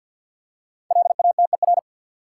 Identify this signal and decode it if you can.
Morse code, heard as a keyed tone, about 53 wpm, 700 Hz. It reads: LATER
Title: Five Letter Words - 50wpm